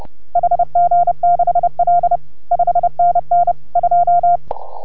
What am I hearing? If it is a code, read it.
Morse code, heard as a keyed tone, about 30 wpm, 690 Hz. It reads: HG6L 5NN 2